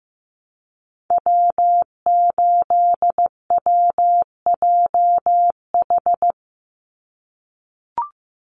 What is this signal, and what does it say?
Morse code, heard as a keyed tone, about 15 wpm, 700 Hz. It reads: W8WJH
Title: USA Call Signs 15wpm